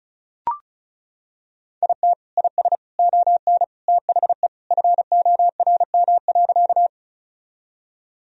Morse code, heard as a keyed tone, about 36 wpm, 700 Hz. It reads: IT IS ON THE FORM.
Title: Sentences from Top 1000 Words - Review Part 1 - 35wpm